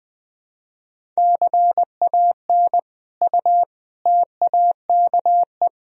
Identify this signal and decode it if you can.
Morse code, heard as a keyed tone, about 21 wpm, 700 Hz. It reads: CAN U TAKE
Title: Sentences from Top 200 Words and Abbreviations 20wpm